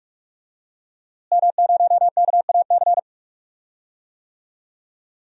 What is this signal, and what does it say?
Morse code, heard as a keyed tone, about 41 wpm, 700 Hz. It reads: M0KAC
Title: DXCC-100 Call Signs - 5 Characters - Repeated 3x - 45wpm